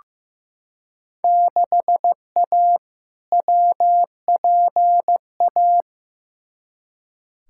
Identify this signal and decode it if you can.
Morse code, heard as a keyed tone, about 15 wpm, 700 Hz. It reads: 6A WPA
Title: Field Day Exchanges Encore - 15wpm